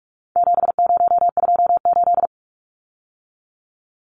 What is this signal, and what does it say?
Morse code, heard as a keyed tone, about 44 wpm, 700 Hz. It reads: 7028
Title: Numbers - 4-Digit 45wpm